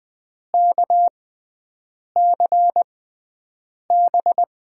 Morse code, heard as a keyed tone, about 20 wpm, spaced as for 7 wpm, 700 Hz. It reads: KCB
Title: USA Call Signs - 6 Characters - 6x Character Spacing - 20wpm